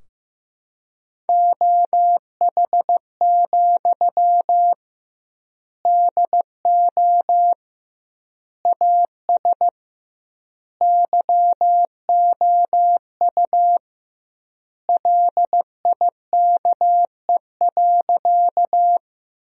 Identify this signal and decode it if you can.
Morse code, heard as a keyed tone, about 15 wpm, 700 Hz. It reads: OH, DO AS YOU LIKE.
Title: 2x Word-Spacing - Sentences from Top 500 Words - 15wpm